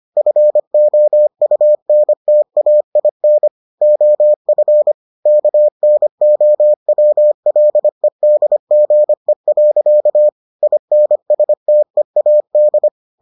Morse code, heard as a keyed tone, about 25 wpm, 600 Hz. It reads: FOUNTAIN OF KNOWLEDGE. INSTEAD